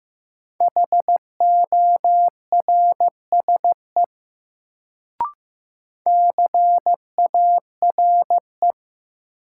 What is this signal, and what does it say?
Morse code, heard as a keyed tone, about 15 wpm, 700 Hz. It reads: HORSE CARE
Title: Sentences from Top 300 Words - New Word Drill - 15wpm